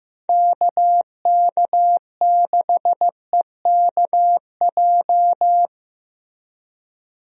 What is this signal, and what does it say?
Morse code, heard as a keyed tone, about 15 wpm, 700 Hz. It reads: KK6EKJ
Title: USA Call Signs - 6 Characters - Repeated 3x - 15wpm